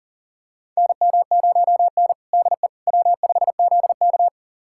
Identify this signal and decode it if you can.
Morse code, heard as a keyed tone, about 40 wpm, 700 Hz. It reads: NM0N DE W5ZK
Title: USA Callsign DE Callsign - 4 Characters - 40wpm